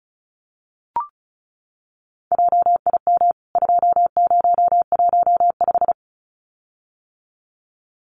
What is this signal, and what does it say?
Morse code, heard as a keyed tone, about 34 wpm, 700 Hz. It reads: JIM 2015
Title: CWT Exchange 35wpm